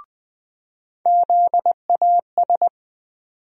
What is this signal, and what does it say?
Morse code, heard as a keyed tone, about 21 wpm, 700 Hz. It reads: ZAS
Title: USA Call Sign Postfixes - 3 Letters 20wpm